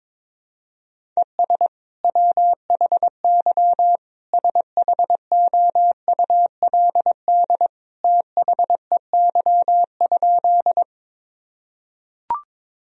Morse code, heard as a keyed tone, about 22 wpm, 700 Hz. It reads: ES WHY SHOULD THEY?
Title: Sentences from Top 300 Words and Abbreviations 22wpm